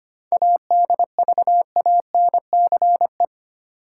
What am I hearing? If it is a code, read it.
Morse code, heard as a keyed tone, about 25 wpm, 700 Hz. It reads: ADVANCE
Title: Sentences from Top 1400 Words - New Word Drill Alt - 25wpm